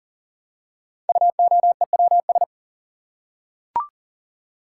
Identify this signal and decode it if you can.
Morse code, heard as a keyed tone, about 39 wpm, 700 Hz. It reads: UOEWS
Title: Code Groups - 5 Letters - 40wpm